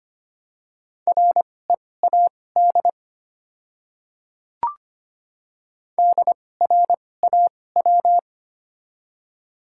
Morse code, heard as a keyed tone, about 25 wpm, spaced as for 18 wpm, 700 Hz. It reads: READ DRAW
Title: 2x Character Spacing - Top 300 Words - 25wpm